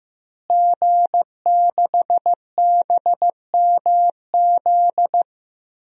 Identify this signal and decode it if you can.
Morse code, heard as a keyed tone, about 15 wpm, 700 Hz. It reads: G6BMZ
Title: DXCC-100 Call Signs - 5 Characters - Repeated 3x - 15wpm